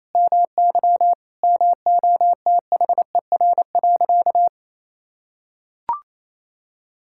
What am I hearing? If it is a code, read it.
Morse code, heard as a keyed tone, about 28 wpm, 700 Hz. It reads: MY MOTHER.
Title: Sentences from Top 1600 Words 28wpm